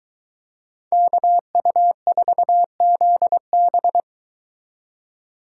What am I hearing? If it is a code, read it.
Morse code, heard as a keyed tone, about 22 wpm, 700 Hz. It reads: KU4ZB